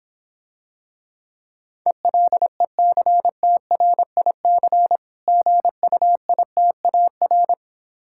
Morse code, heard as a keyed tone, about 26 wpm, 700 Hz. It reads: ELECTRIC GUITAR